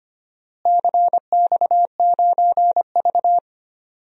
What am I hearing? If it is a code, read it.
Morse code, heard as a keyed tone, about 25 wpm, 700 Hz. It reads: CX9V
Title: DXCC-100 Call Signs - 4 Characters - Repeated 2x - 25wpm